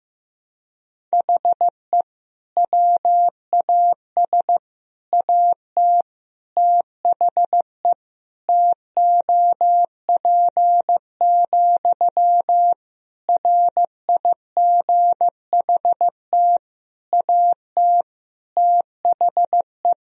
Morse code, heard as a keyed tone, about 15 wpm, 700 Hz. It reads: HE WAS AT THE TOP, RIGHT AT THE
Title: Sentences from Top 1000 Words - Review Part 2 - 15wpm